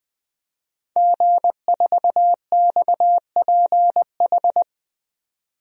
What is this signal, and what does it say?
Morse code, heard as a keyed tone, about 20 wpm, 700 Hz. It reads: G4XPH
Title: DXCC-100 Call Signs - 5 Characters - 20wpm